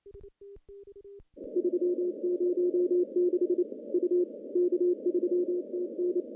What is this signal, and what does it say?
Morse code, heard as a keyed tone, about 28 wpm, 360 Hz. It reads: 306 U K3TN
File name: contests / CW-SS-k5tr- -1500